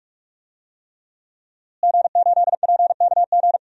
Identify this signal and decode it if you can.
Morse code, heard as a keyed tone, about 46 wpm, 700 Hz. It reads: G8PKG
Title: DXCC-100 Call Signs - 5 Characters - Repeated 3x - 45wpm